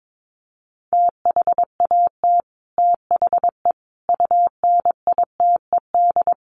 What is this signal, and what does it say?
Morse code, heard as a keyed tone, about 22 wpm, 700 Hz. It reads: THAT THE UNITED
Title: Sets of 5 Words Encore 22wpm